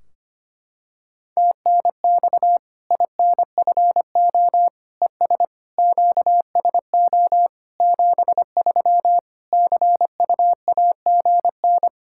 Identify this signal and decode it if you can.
Morse code, heard as a keyed tone, about 25 wpm, 700 Hz. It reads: TNX INFO ES QSO 73 CUAGN
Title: Practice QSO - Exchange 4 - Ending - 25wpm